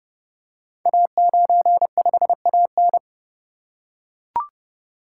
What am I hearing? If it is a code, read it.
Morse code, heard as a keyed tone, about 30 wpm, 700 Hz. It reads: A95AN
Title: Code Groups - 5 Letter:Numbers - 30wpm